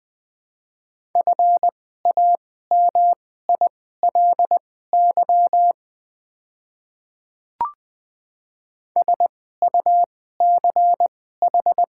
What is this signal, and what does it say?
Morse code, heard as a keyed tone, about 20 wpm, spaced as for 14 wpm, 700 Hz. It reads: FAMILY SUCH
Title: 2x Character Spacing - Top 500 Words - 20wpm